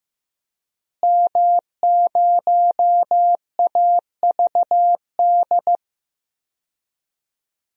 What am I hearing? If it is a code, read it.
Morse code, heard as a keyed tone, about 15 wpm, 700 Hz. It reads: M0AVD
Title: DXCC-100 Call Signs - 5 Characters - Repeated 2x - 15wpm